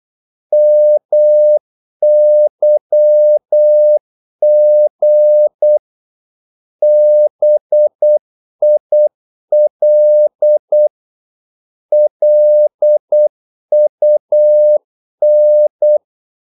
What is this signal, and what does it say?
Morse code, heard as a keyed tone, about 8 wpm, 600 Hz. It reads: MYG BIL LUN
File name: Korte ord DK 8wpm